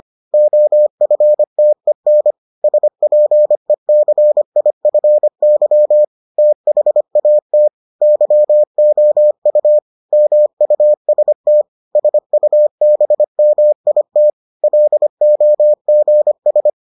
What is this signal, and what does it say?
Morse code, heard as a keyed tone, about 25 wpm, 600 Hz. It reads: OFTEN SPECIFY THAT YOU MUST SUBMIT LOGS